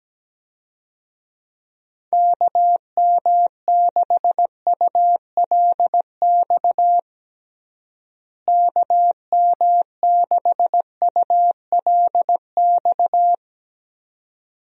Morse code, heard as a keyed tone, about 17 wpm, 700 Hz. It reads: KM6ULX KM6ULX
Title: USA Call Signs - 6 Characters - Repeated 2x - 17wpm